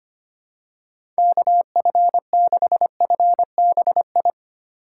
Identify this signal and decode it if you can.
Morse code, heard as a keyed tone, about 25 wpm, 700 Hz. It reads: KF6FBI
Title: USA Call Signs - 6 Characters - 2x Character Spacing - 25wpm